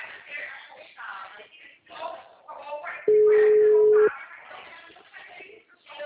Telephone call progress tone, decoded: ringback tone

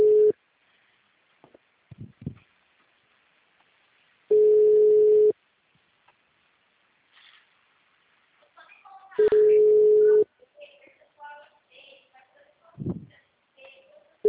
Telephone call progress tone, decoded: ringback tone

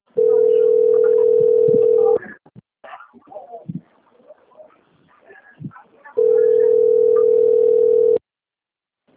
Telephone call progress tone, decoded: ringback tone

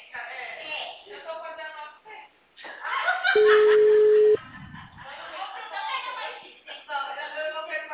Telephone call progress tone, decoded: ringback tone

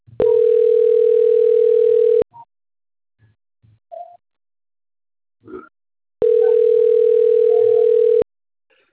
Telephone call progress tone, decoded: ringback tone

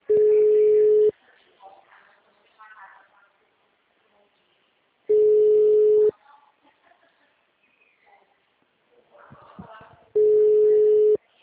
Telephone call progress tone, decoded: ringback tone

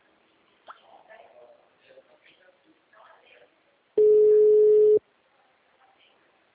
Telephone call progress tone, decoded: ringback tone

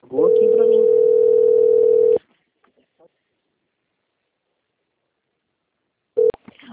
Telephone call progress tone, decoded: ringback tone

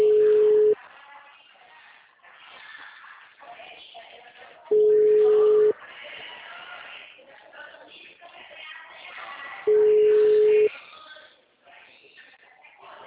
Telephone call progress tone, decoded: ringback tone

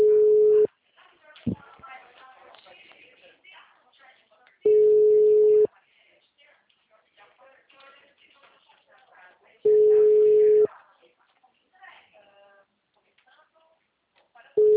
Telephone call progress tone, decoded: ringback tone

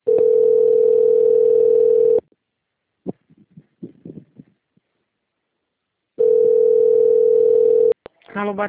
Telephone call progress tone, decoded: ringback tone

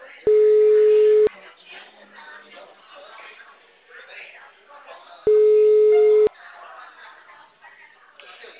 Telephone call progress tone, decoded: ringback tone

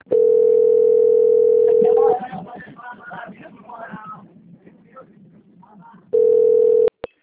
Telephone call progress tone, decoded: ringback tone

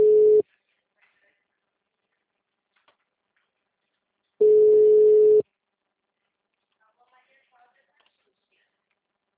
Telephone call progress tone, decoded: ringback tone

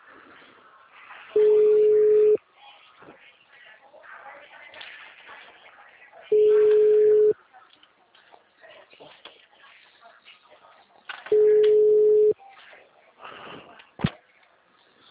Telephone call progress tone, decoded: ringback tone

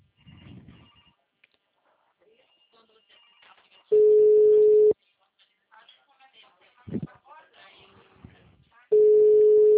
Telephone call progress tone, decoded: ringback tone

